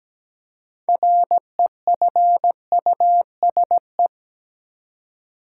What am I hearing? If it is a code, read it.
Morse code, heard as a keyed tone, about 17 wpm, 700 Hz. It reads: REFUSE
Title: Six Letter Words - 17wpm